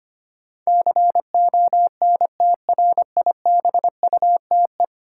Morse code, heard as a keyed tone, about 25 wpm, 700 Hz. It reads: CONTRIBUTE
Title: Sentences from Top 1400 Words - New Word Drill Alt - 25wpm